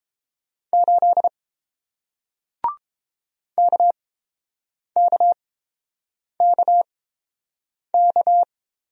Morse code, heard as a keyed tone, about 30 wpm, spaced as for 7 wpm, 700 Hz. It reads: 8 KKKK